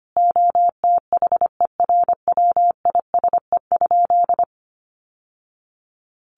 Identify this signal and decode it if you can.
Morse code, heard as a keyed tone, about 25 wpm, 700 Hz. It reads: OTHERWISE?
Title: Sentences from Top 1200 Words 25wpm